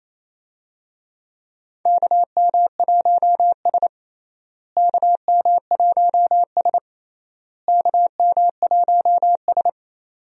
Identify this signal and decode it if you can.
Morse code, heard as a keyed tone, about 28 wpm, 700 Hz. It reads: KM1S KM1S KM1S
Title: USA Call Signs - 4 Characters - Repeated 3x - 28wpm